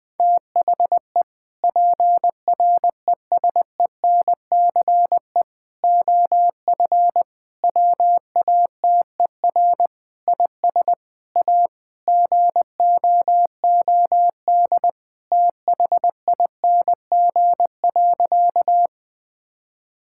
Morse code, heard as a keyed tone, about 20 wpm, 700 Hz. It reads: THE PRESENCE OF WATER IS A GOOD THING.
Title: Sentences with 3-Word Phrases - 20wpm